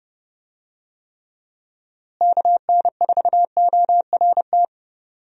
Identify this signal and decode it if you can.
Morse code, heard as a keyed tone, about 32 wpm, 700 Hz. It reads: KN4ORT
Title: USA Call Signs - 6 Characters - Repeated 2x - 30wpm